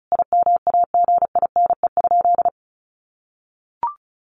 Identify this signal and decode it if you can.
Morse code, heard as a keyed tone, about 34 wpm, 700 Hz. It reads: IMAGINE?
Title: Sentences from Top 1400 Words 35wpm